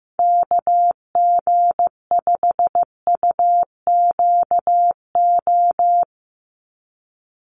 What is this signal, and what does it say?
Morse code, heard as a keyed tone, about 15 wpm, 700 Hz. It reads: KG5UQO
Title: USA Call Signs 15wpm